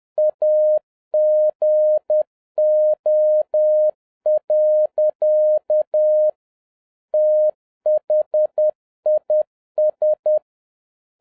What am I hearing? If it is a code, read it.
Morse code, heard as a keyed tone, about 10 wpm, 610 Hz. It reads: AGO. THIS